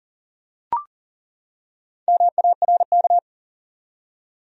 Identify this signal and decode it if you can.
Morse code, heard as a keyed tone, about 43 wpm, 700 Hz. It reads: MARK
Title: USA Common Names - Max 4 Letters - 40wpm